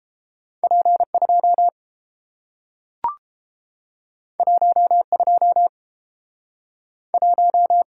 Morse code, heard as a keyed tone, about 30 wpm, 700 Hz. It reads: P 2 1 2 1